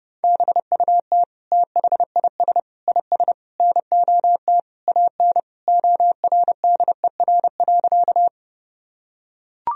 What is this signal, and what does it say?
Morse code, heard as a keyed tone, about 32 wpm, 700 Hz. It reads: BUT THIS IS NOT AN ORDER.
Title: Sentences from Top 1000 Words - Review Part 2 - 30wpm